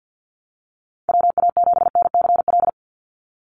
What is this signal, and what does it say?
Morse code, heard as a keyed tone, about 50 wpm, 700 Hz. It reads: WA7NCL
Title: NA SOTA Activator Callsigns 50wpm